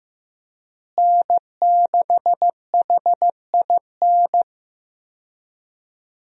Morse code, heard as a keyed tone, about 15 wpm, 700 Hz. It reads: N6HIN
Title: USA Call Signs - 5 Characters - Repeated 2x - 15wpm